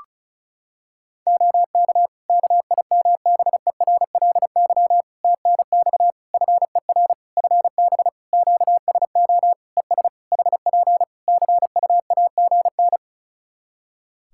Morse code, heard as a keyed tone, about 36 wpm, 700 Hz. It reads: OK KIMBERLY TNX FER FB QSO ES HP CUAGN
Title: Practice QSO - Exchange 4 - Ending - 35wpm